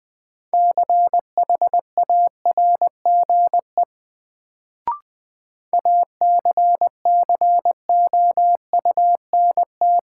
Morse code, heard as a keyed tone, about 20 wpm, 700 Hz. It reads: CHARGE ACCOUNT
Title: Sentences from Top 900 Words - New Word Drill - 20wpm